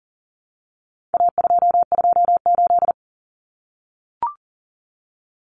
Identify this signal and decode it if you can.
Morse code, heard as a keyed tone, about 40 wpm, 700 Hz. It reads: A228
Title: Common Cut Numbers - 4-Digit 40wpm